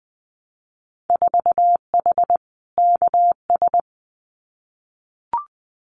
Morse code, heard as a keyed tone, about 20 wpm, 700 Hz. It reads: 4H KS
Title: Winter Field Day Exchanges Encore 20wpm